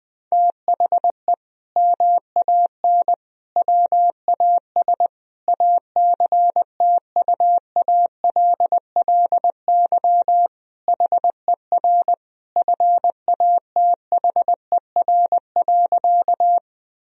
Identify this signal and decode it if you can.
Morse code, heard as a keyed tone, about 20 wpm, 700 Hz. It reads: THE MAN WAS ACTUALLY HER FATHER.
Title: Sentences from Top 1000 Words - Review Part 4 - 20wpm